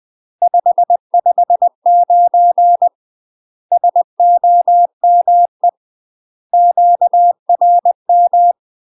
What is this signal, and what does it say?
Morse code, heard as a keyed tone, about 20 wpm, 700 Hz. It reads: 559 SOME QRM
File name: UK MASTER QSOs TRAINING / 20WPM / UK MASTER QSOs 0052